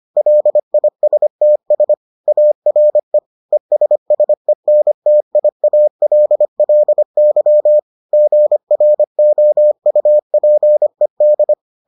Morse code, heard as a keyed tone, about 25 wpm, 600 Hz. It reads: LISTS ARE ESSENTIALLY GROUPED